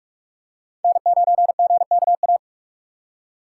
Morse code, heard as a keyed tone, about 45 wpm, 690 Hz. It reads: N9GKA
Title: USA Call Signs - 5 Characters - 6x Character Spacing - 45wpm